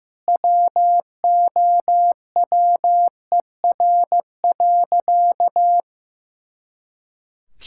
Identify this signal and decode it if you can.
Morse code, heard as a keyed tone, about 15 wpm, 700 Hz. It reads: WOWER.
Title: Pangram Challenge v2 - 15wpm